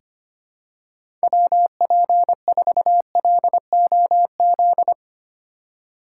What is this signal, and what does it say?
Morse code, heard as a keyed tone, about 25 wpm, 700 Hz. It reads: WP4LOZ
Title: USA Call Signs - 6 Characters - 3x Character Spacing - 25wpm